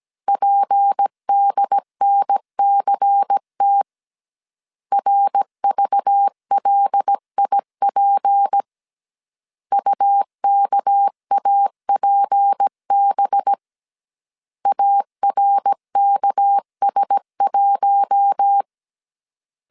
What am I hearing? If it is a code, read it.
Morse code, heard as a keyed tone, about 17 wpm, 800 Hz. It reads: PDNCT RVLIP UKAPB ARKS1
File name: 16wpm